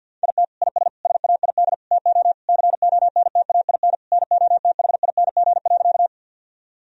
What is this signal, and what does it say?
Morse code, heard as a keyed tone, about 50 wpm, 700 Hz. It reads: IT IS SAID TO CONTAIN NOTHING.
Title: Sentences from Top 1000 Words - Review Part 2 - 50wpm